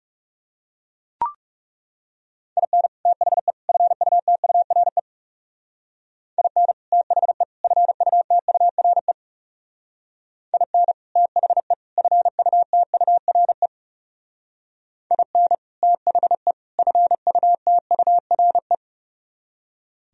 Morse code, heard as a keyed tone, about 34 wpm, 700 Hz. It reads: IN THE FUTURE IN THE FUTURE IN THE FUTURE IN THE FUTURE
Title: Speed-Racing - Sets of 3 Words 30wpm